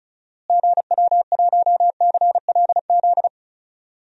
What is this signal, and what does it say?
Morse code, heard as a keyed tone, about 36 wpm, 700 Hz. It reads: GW1CLZ
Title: DXCC-100 Call Signs - 6 Characters - Repeated 3x - 35wpm